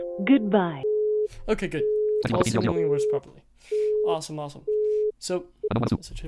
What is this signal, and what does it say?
Busy tone (Europe, 425 Hz), with speech.